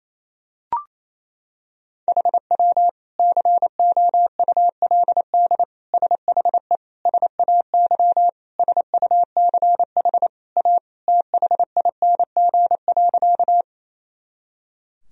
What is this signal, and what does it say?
Morse code, heard as a keyed tone, about 28 wpm, 700 Hz. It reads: HW COULD SHE SAY SUCH A THING.
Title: Sentences from Top 200 Words and Abbreviations 28wpm